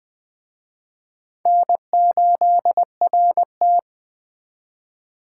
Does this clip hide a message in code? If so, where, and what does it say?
Yes: Morse code from 1.5 to 3.8 s, heard as a keyed tone, about 20 wpm, 700 Hz. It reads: N8RT